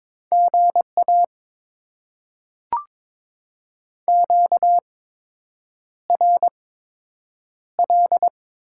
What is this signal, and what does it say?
Morse code, heard as a keyed tone, about 22 wpm, spaced as for 8 wpm, 700 Hz. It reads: GA QRL
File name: Top 20 QSO Elements - 8x Character Spacing - 22wpm